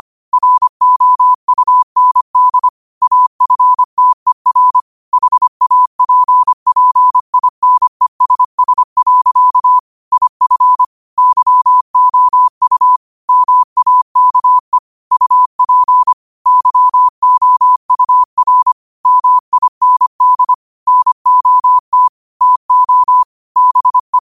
Morse code, heard as a keyed tone, about 25 wpm, 1000 Hz. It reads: ROUND AFTER HAPPINESS. IF YOU MAKE UP YOUR MIND NOT TO BE
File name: qotdfiles / 25wpm